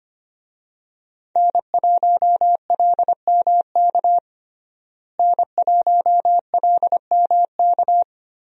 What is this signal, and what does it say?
Morse code, heard as a keyed tone, about 25 wpm, 700 Hz. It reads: N1LMK N1LMK
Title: USA Call Signs - 5 Characters - Repeated 2x - 25wpm